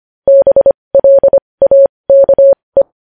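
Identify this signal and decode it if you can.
Morse code, heard as a keyed tone, about 25 wpm, spaced as for 20 wpm, 550 Hz. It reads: BLAKE
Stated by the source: Morse code at 20 wpm.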